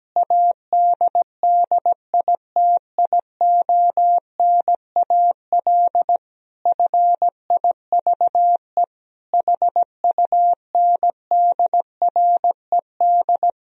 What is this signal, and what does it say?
Morse code, heard as a keyed tone, about 17 wpm, 700 Hz. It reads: ADDITIONAL FIVE HUNDRED
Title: Sentences from Top 1000 Words - Review Part 4 - 17wpm